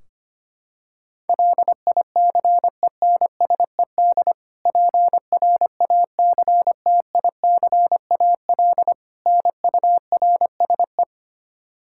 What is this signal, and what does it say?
Morse code, heard as a keyed tone, about 25 wpm, 700 Hz. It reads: LICENSED PRACTICAL NURSE
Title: Practice QSO - Exchange 3 - More Info - 25wpm